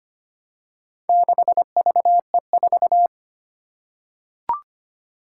Morse code, heard as a keyed tone, about 25 wpm, 700 Hz. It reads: 6VE4
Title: Four Challenging Letter-Number Combinations 25wpm